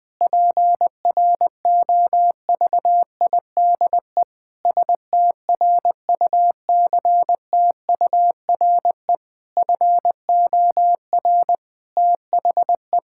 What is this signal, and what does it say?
Morse code, heard as a keyed tone, about 19 wpm, 700 Hz. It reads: PROVIDE STRUCTURE FOR THE
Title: Sentences from Top 1000 Words - Review Part 6 - 20wpm